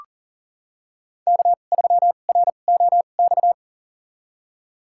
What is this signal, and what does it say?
Morse code, heard as a keyed tone, about 40 wpm, spaced as for 29 wpm, 700 Hz. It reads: K3ROX
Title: USA Call Signs - 5 Characters - 2x Character Spacing - 40wpm